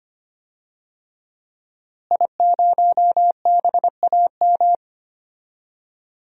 Morse code, heard as a keyed tone, about 25 wpm, 700 Hz. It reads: I0BAM
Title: DXCC-100 Call Signs - 5 Characters - Repeated 3x - 25wpm